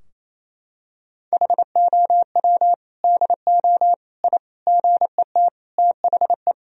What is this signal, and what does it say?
Morse code, heard as a keyed tone, about 28 wpm, 700 Hz. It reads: HOW DO I GET THE
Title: Sentences from Top 1000 Words - Review Part 3 - 28wpm